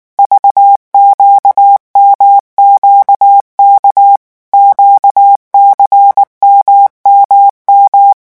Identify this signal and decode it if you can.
Morse code, heard as a keyed tone, about 19 wpm, 800 Hz. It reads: VQMQK QCMMM
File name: morse mp3 / revision2-vitesse 18 mots